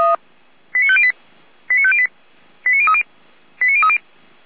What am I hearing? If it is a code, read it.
Five-tone selective call (ZVEI), heard as quick runs of five tones, tones of about 70 ms. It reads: 89508 89508 89030 89030